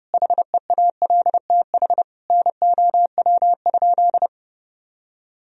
Morse code, heard as a keyed tone, about 30 wpm, 700 Hz. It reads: HEALTH NOW?
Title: Sentences from Top 1000 Words - Review Part 3 - 30wpm